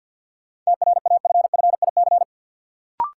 Morse code, heard as a keyed tone, about 46 wpm, 700 Hz. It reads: TRAFFIC
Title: Sentences from Top 1300 Words - New Word Drill Alt - 50wpm